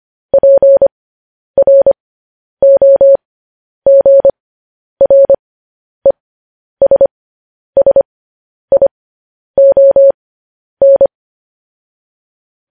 Morse code, heard as a keyed tone, about 25 wpm, spaced as for 10 wpm, 550 Hz. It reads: PROGRESSION
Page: news headlines in morse code at 10 wpm